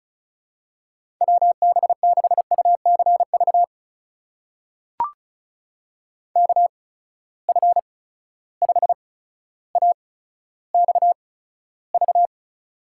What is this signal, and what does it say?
Morse code, heard as a keyed tone, about 36 wpm, spaced as for 15 wpm, 700 Hz. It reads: WB6UCV KF5AXV
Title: USA Call Signs - 6 Characters - 8x Character Spacing - 35wpm